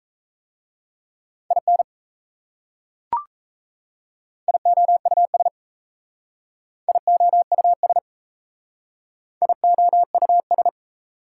Morse code, heard as a keyed tone, about 35 wpm, 700 Hz. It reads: IN IOUS IOUS IOUS